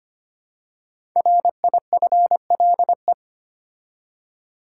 Morse code, heard as a keyed tone, about 25 wpm, 700 Hz. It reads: RIFLE
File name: 5 Letter Words - First 3 Letters Given - 25wpm